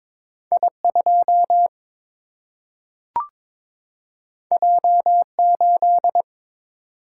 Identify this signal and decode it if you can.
Morse code, heard as a keyed tone, about 22 wpm, 700 Hz. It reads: I 2 J 8
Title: Single-Letter Single-Number 22wpm